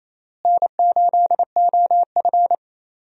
Morse code, heard as a keyed tone, about 28 wpm, 700 Hz. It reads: N8OF
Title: USA Call Signs - 4 Characters - Repeated 3x - 28wpm